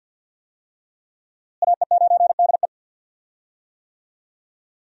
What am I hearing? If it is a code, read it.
Morse code, heard as a keyed tone, about 52 wpm, 700 Hz. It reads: AE9DE
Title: USA Call Signs - 5 Characters - Repeated 2x - 50wpm